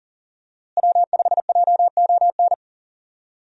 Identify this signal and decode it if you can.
Morse code, heard as a keyed tone, about 39 wpm, 700 Hz. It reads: W5JON
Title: USA Call Signs - 5 Characters - Repeated 2x - 40wpm